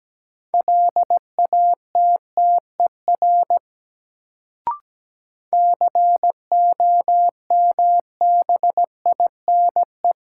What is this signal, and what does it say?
Morse code, heard as a keyed tone, about 17 wpm, 700 Hz. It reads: LATTER COMBINE
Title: Sentences from Top 1600 Words - New Word Drill Alt - 17wpm